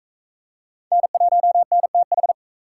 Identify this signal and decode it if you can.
Morse code, heard as a keyed tone, about 42 wpm, 700 Hz. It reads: N1NTH